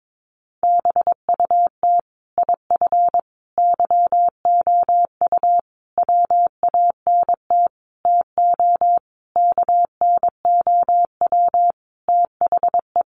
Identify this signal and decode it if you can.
Morse code, heard as a keyed tone, about 22 wpm, 700 Hz. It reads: BUT IF YOU WANT TO KNOW THE
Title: Sentences from Top 800 Words 22wpm